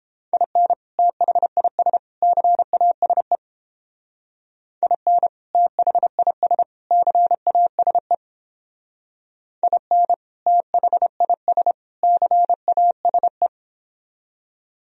Morse code, heard as a keyed tone, about 31 wpm, 700 Hz. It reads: IN THIS CASE IN THIS CASE IN THIS CASE